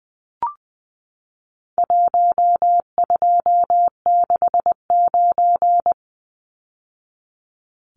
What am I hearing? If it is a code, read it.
Morse code, heard as a keyed tone, about 20 wpm, 700 Hz. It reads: 1269
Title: Numbers - Contest Style - 4-Digit 20wpm